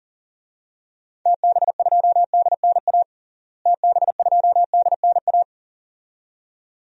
Morse code, heard as a keyed tone, about 40 wpm, 700 Hz. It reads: TB2DNA TB2DNA
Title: DXCC-100 Call Signs - 6 Characters - Repeated 2x - 40wpm